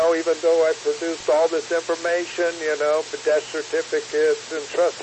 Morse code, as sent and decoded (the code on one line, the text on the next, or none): none